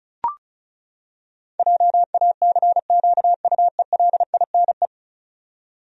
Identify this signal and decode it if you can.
Morse code, heard as a keyed tone, about 36 wpm, 700 Hz. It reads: JACQUELINE
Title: USA Common Names - 35wpm